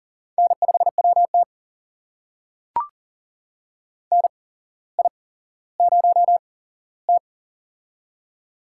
Morse code, heard as a keyed tone, about 39 wpm, spaced as for 17 wpm, 700 Hz. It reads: N5WT NI0T
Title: USA Call Signs - 4 Characters - 8x Character Spacing - 40wpm